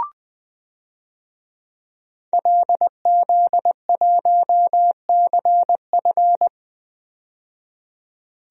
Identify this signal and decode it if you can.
Morse code, heard as a keyed tone, about 20 wpm, 700 Hz. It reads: LZ1CF
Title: DXCC-100 Call Signs - 5 Characters - 20wpm